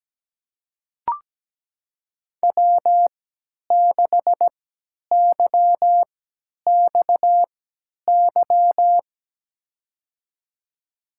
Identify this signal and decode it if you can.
Morse code, heard as a keyed tone, about 17 wpm, spaced as for 10 wpm, 700 Hz. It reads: W6YXY
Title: USA Call Signs - 5 Characters - 3x Character Spacing - 17wpm